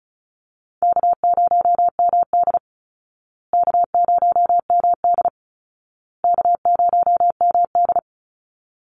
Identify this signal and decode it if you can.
Morse code, heard as a keyed tone, about 34 wpm, 700 Hz. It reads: K0MD K0MD K0MD
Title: Call Signs Repeated 2x - 35wpm